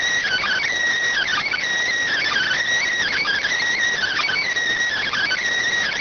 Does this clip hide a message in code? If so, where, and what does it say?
Yes: Emergency Alert System (SAME) tones (header bursts) from 1.1 to 5.4 s, heard as harsh buzzing bursts.